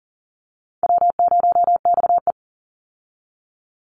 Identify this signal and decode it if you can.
Morse code, heard as a keyed tone, about 40 wpm, 700 Hz. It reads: W0XE
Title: Easy Call Signs Repeated 2x - 40wpm